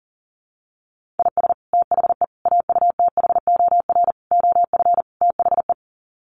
Morse code, heard as a keyed tone, about 40 wpm, 700 Hz. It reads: IS THE AUTHOR OF THE
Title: Sets of 5 Words Encore 40wpm